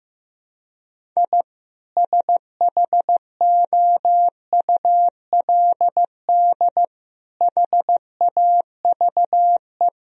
Morse code, heard as a keyed tone, about 16 wpm, 700 Hz. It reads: I SHOULD HAVE
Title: Sentences from Top 1000 Words - Review Part 1 - 15wpm